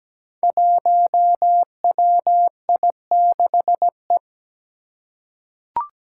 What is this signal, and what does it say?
Morse code, heard as a keyed tone, about 17 wpm, 700 Hz. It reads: 1WI6E
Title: Five Challenging Letter-Number Combinations 17wpm